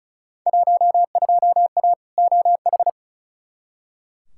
Morse code, heard as a keyed tone, about 34 wpm, 700 Hz. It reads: 12A OH
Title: Field Day Exchanges Encore - 35wpm